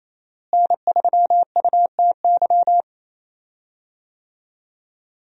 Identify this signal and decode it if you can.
Morse code, heard as a keyed tone, about 27 wpm, 700 Hz. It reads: N3UTY
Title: USA Call Signs - 5 Characters - Repeated 2x - 28wpm